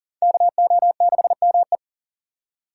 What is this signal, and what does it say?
Morse code, heard as a keyed tone, about 51 wpm, 700 Hz. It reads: KO6ME